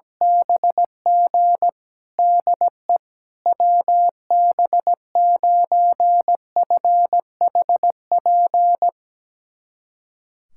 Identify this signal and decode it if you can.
Morse code, heard as a keyed tone, about 17 wpm, 700 Hz. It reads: BG DE WB9FHP
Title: USA Callsign DE Callsign - 6 Characters - 17wpm